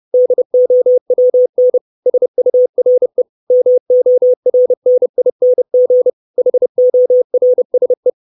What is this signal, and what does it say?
Morse code, heard as a keyed tone, about 30 wpm, 500 Hz. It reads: DOWN SURE MORNING HORSE
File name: 632 Words 30WPM